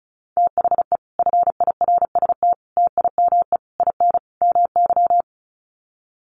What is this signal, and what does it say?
Morse code, heard as a keyed tone, about 34 wpm, 700 Hz. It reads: THE FIRST TIME IN MY
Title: Sets of 5 Words Encore 35wpm